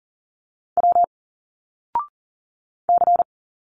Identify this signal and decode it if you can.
Morse code, heard as a keyed tone, about 40 wpm, 700 Hz. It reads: W C
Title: Single Letters 40wpm